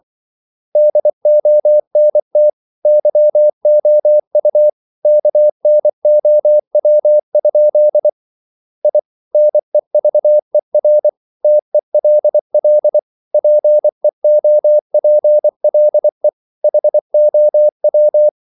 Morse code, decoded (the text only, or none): DONT YOU KNOW? I NEVER TELL PEOPLE HOW